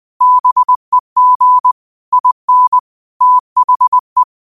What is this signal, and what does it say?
Morse code, heard as a keyed tone, about 20 wpm, 1000 Hz. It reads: BEG IN THE